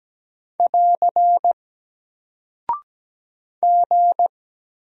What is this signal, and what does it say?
Morse code, heard as a keyed tone, about 17 wpm, 700 Hz. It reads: <AR> G
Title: Single Character (Letter:Number:Punctuation:Prosign) Alt 17wpm